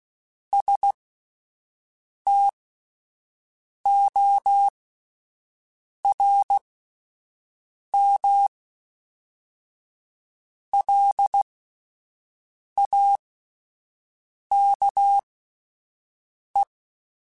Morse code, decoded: STORM LAKE

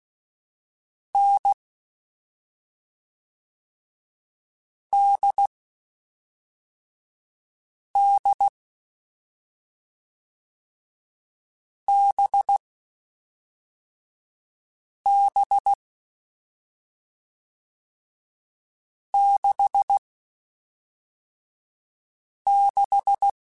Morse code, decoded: N D D B B 6 6

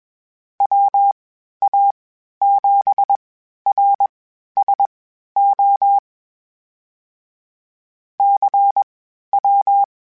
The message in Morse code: WA7RSO CW

